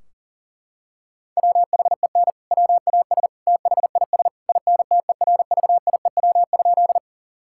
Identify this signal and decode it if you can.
Morse code, heard as a keyed tone, about 40 wpm, 700 Hz. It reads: WHEN WAS THIS INTERVIEW?
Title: Sentences from Top 1600 Words 40wpm